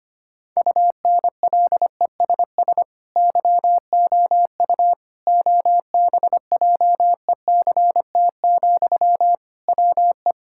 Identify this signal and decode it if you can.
Morse code, heard as a keyed tone, about 25 wpm, 700 Hz. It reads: UNLESS YOU OBJECT, WE
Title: Sentences from Top 1000 Words - Review Part 6 - 25wpm